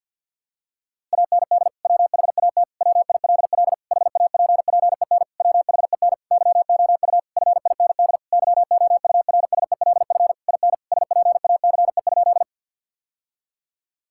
Morse code, heard as a keyed tone, about 50 wpm, 700 Hz. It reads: AND WHAT WILL HAPPEN WHEN YOU FIND YOURSELF IN SPACE?
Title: Sentences from Top 1000 Words - Review Part 5 - 50wpm